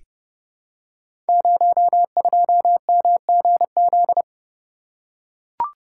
Morse code, heard as a keyed tone, about 30 wpm, 700 Hz. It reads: 02MGZ
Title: Code Groups - 5 Letter:Numbers - 30wpm